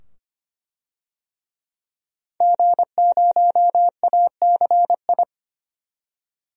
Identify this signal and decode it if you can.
Morse code, heard as a keyed tone, about 25 wpm, 700 Hz. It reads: G0ACI
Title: DXCC-100 Call Signs - 5 Characters - 25wpm